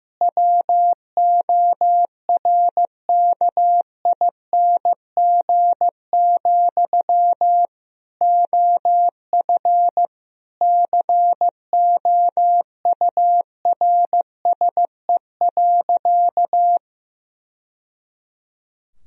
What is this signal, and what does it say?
Morse code, heard as a keyed tone, about 15 wpm, 700 Hz. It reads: WORKING, OF COURSE.
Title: Sentences from Top 1000 Words - Review Part 4 - 15wpm